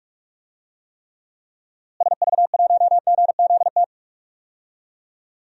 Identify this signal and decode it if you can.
Morse code, heard as a keyed tone, about 44 wpm, 700 Hz. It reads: SV1GZT
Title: DXCC-100 Call Signs - 6 Characters - Repeated 3x - 45wpm